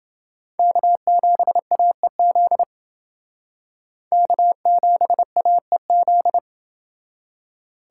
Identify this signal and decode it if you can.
Morse code, heard as a keyed tone, about 29 wpm, 700 Hz. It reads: K7AEZ K7AEZ